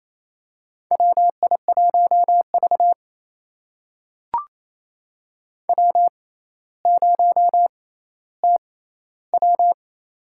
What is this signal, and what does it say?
Morse code, heard as a keyed tone, about 28 wpm, spaced as for 14 wpm, 700 Hz. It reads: WI1V W0TW
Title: USA Call Signs - 4 Characters - 6x Character Spacing - 28wpm